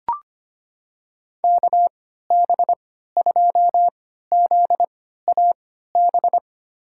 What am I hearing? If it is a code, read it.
Morse code, heard as a keyed tone, about 25 wpm, spaced as for 14 wpm, 700 Hz. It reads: KB2ZAB
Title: USA Call Signs - 6 Characters - 3x Character Spacing - 25wpm